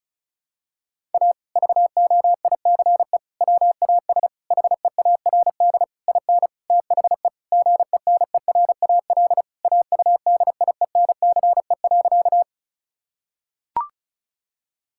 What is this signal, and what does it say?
Morse code, heard as a keyed tone, about 36 wpm, 700 Hz. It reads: A VOICE WAS HEARD IN THE GENERAL AUDIENCE.
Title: Sentences from Top 1000 Words - Review Part 5 - 35wpm